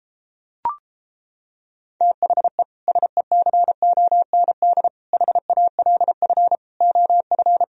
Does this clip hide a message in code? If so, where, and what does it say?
Yes: Morse code from 2.0 to 7.6 s, heard as a keyed tone, about 31 wpm, 700 Hz. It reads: THE SECOND HALF OF